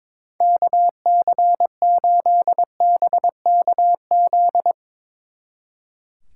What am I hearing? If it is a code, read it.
Morse code, heard as a keyed tone, about 22 wpm, 700 Hz. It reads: KC8BKZ
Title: Speed Racing - USA Call Signs 22wpm